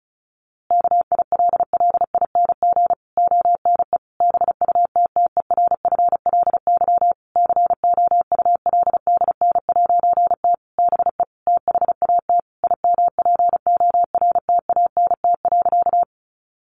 Morse code, heard as a keyed tone, about 34 wpm, 700 Hz. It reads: KILLING ONE BUTTERFLY COULDN'T BE THAT IMPORTANT.
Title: A Sound Of Thunder 35wpm